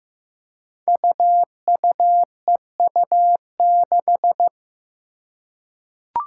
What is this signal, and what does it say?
Morse code, heard as a keyed tone, about 15 wpm, 700 Hz. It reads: UUEU6
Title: Five Challenging Letter-Number Combinations 15wpm